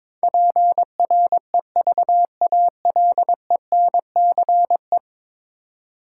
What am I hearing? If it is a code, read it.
Morse code, heard as a keyed tone, about 21 wpm, 700 Hz. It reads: PREVALENCE
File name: practice / 22wpm / 10 Letter Words - First 3 Letters Given - 22wpm